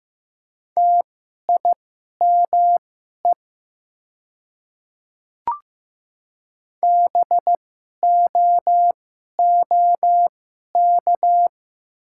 Morse code, heard as a keyed tone, about 15 wpm, spaced as for 11 wpm, 700 Hz. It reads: TIME BOOK
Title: 2x Character Spacing - Top 500 Words - 15wpm